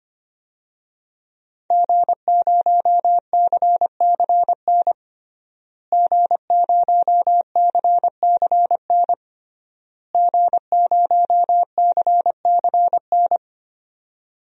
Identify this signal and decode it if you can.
Morse code, heard as a keyed tone, about 25 wpm, 700 Hz. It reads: G0CCN G0CCN G0CCN
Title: DXCC-100 Call Signs - 5 Characters - Repeated 3x - 25wpm